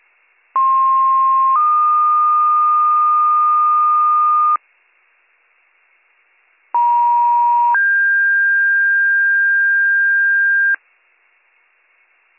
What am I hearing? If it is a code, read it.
Two-tone page, heard as a one-second tone, then a three-second tone, about 1060 then 1220 Hz, about 950 then 1600 Hz. It is heard from 0.6 to 10.8 s.